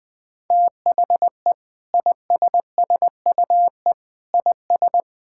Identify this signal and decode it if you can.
Morse code, heard as a keyed tone, about 21 wpm, 700 Hz. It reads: THE ISSUE IS
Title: Sentences from Top 1000 Words - Review Part 6 - 20wpm